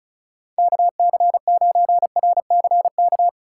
Morse code, heard as a keyed tone, about 34 wpm, 700 Hz. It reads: KC9RCK
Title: USA Call Signs - 6 Characters - 3x Character Spacing - 35wpm